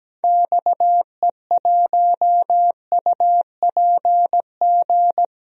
Morse code, heard as a keyed tone, about 17 wpm, 700 Hz. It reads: XE1UPG
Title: DXCC-100 Call Signs - 6 Characters - 17wpm